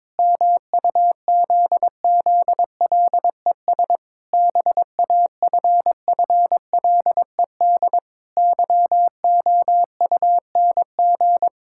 Morse code, heard as a keyed tone, about 22 wpm, 700 Hz. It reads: MUZZLES BAFFLED YOUNG
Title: Pangram Challenge v2 - 22wpm